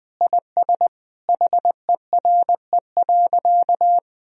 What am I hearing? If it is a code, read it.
Morse code, heard as a keyed tone, about 21 wpm, 700 Hz. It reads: IS HERE.
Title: Sentences from Top 1000 Words - Review Part 3 - 20wpm